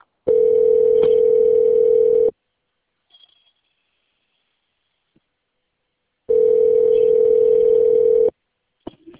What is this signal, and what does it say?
Ringback tone (North America), 440 + 480 Hz.